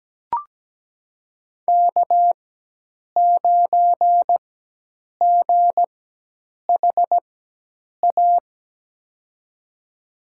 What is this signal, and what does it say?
Morse code, heard as a keyed tone, about 17 wpm, spaced as for 8 wpm, 700 Hz. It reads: K9GHA
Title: USA Call Signs - 5 Characters - 4x Character Spacing - 17wpm